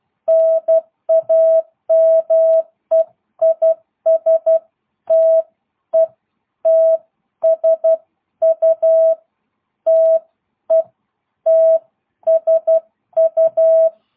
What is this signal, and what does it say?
Morse code, heard as a keyed tone, about 12 wpm, spaced as for 10 wpm, 650 Hz. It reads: NAME IS TETSU TETSU
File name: jh8cbh / 2022cw